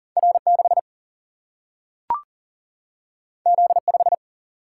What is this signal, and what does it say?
Morse code, heard as a keyed tone, about 40 wpm, 700 Hz. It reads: R 6 Z 5
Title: Single-Letter Single-Number 40wpm